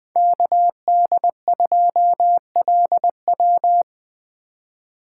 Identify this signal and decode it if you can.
Morse code, heard as a keyed tone, about 24 wpm, 700 Hz. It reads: KD2LW
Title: Speed Racing - USA Call Signs 20wpm